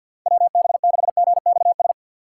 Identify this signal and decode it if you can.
Morse code, heard as a keyed tone, about 48 wpm, 700 Hz. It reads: WB6GXS
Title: USA Call Signs - 6 Characters - 4x Character Spacing - 50wpm